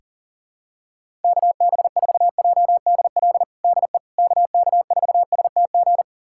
Morse code, heard as a keyed tone, about 37 wpm, 700 Hz. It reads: KB4JDL DE KK4STG
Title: USA Callsign DE Callsign 40wpm